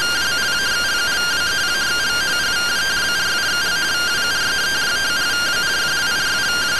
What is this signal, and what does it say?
Two-tone teleprinter signal, about 170 Hz shift.